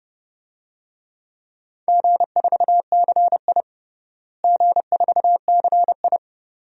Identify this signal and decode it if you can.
Morse code, heard as a keyed tone, about 30 wpm, 700 Hz. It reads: G4CI G4CI
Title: DXCC-100 Call Signs - 4 Characters - Repeated 2x - 30wpm